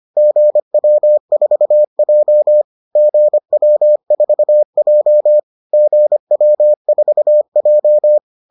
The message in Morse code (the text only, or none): GW4J GW4J GW4J